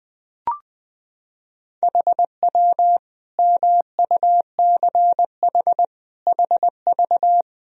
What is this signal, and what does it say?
Morse code, heard as a keyed tone, about 20 wpm, 700 Hz. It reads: HW MUCH HV
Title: Sentences from Top 300 Words and Abbreviations 20wpm